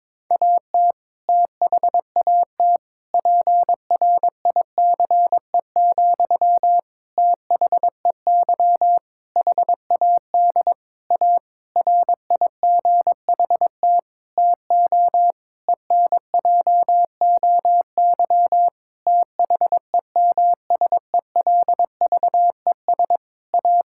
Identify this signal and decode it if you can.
Morse code, heard as a keyed tone, about 22 wpm, 700 Hz. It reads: AT THAT PRICE, THEY HAD A RIGHT TO ENJOY THEMSELVES A
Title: Sentences from Top 1000 Words - Review Part 5 - 22wpm